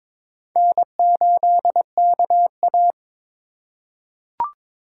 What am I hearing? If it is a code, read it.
Morse code, heard as a keyed tone, about 22 wpm, 700 Hz. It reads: N8KA
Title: USA Call Signs - 4 Characters - 2x Character Spacing - 22wpm